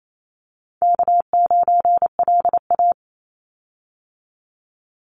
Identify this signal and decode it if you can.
Morse code, heard as a keyed tone, about 28 wpm, 700 Hz. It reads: K9LA
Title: Easy Call Signs - 28wpm